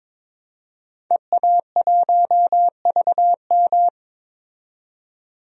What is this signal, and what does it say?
Morse code, heard as a keyed tone, about 22 wpm, 700 Hz. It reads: EA1VM
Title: DXCC-100 Call Signs - 5 Characters - 22wpm